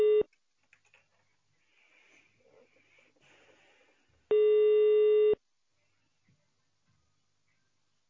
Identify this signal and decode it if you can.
Ringback tone (Europe, 425 Hz).